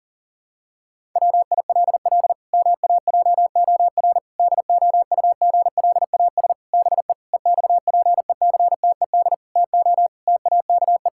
Morse code, heard as a keyed tone, about 40 wpm, 700 Hz. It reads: WILL MAJOR DOUGLAS BE EXPECTED TO TAKE
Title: Pangram Challenge v2 - 40wpm